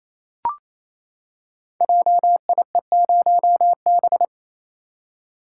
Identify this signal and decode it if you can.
Morse code, heard as a keyed tone, about 27 wpm, 700 Hz. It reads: JIE0B
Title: Code Groups - 5 Letter:Numbers - 28wpm